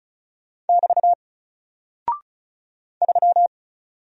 Morse code, heard as a keyed tone, about 34 wpm, 700 Hz. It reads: <BT> 3
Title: Single Character (Letter:Number:Punctuation:Prosign) Alt 35wpm